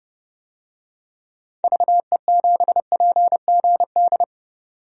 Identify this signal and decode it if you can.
Morse code, heard as a keyed tone, about 31 wpm, 700 Hz. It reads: VE7PGD
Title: DXCC-100 Call Signs - 6 Characters - 30wpm